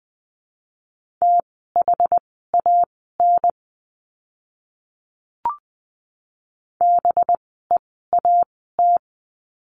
Morse code, heard as a keyed tone, about 20 wpm, spaced as for 14 wpm, 700 Hz. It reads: THAN BEAT
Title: 185 Lesson 30f - Letter b - Words 20wpm